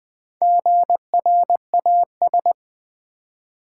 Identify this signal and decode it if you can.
Morse code, heard as a keyed tone, about 20 wpm, 700 Hz. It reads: GRAS
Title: Build A Word - 1 Letter At A Time - 3 to 5-Letter Words - 20wpm